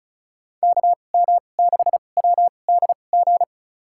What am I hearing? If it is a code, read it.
Morse code, heard as a keyed tone, about 36 wpm, spaced as for 25 wpm, 700 Hz. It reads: KM6WDG
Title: USA Call Signs - 6 Characters - 2x Character Spacing - 35wpm